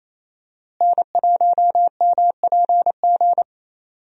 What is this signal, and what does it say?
Morse code, heard as a keyed tone, about 28 wpm, 700 Hz. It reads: N1MPG
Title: USA Call Signs - 5 Characters - Repeated 2x - 28wpm